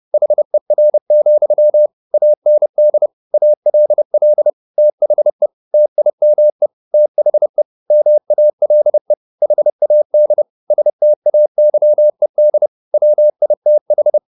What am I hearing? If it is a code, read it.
Morse code, heard as a keyed tone, about 30 wpm, 600 Hz. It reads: HER, AND ALL THE TIME THE MALE HAD STAYED WITH
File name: the old man and the sea / the old man and the sea 1 30wpm